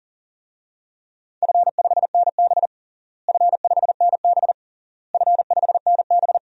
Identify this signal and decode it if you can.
Morse code, heard as a keyed tone, about 37 wpm, 700 Hz. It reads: F5NB F5NB F5NB
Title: DXCC-100 Call Signs - 4 Characters - Repeated 3x - 40wpm